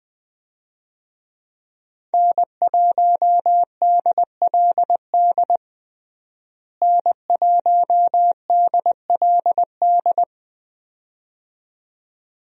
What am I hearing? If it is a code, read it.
Morse code, heard as a keyed tone, about 20 wpm, 700 Hz. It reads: N1DLD N1DLD
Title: USA Call Signs - 5 Characters - Repeated 2x - 20wpm